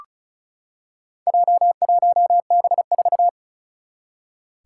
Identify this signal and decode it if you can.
Morse code, heard as a keyed tone, about 36 wpm, 700 Hz. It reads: J1B4
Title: Four Challenging Letter-Number Combinations 35wpm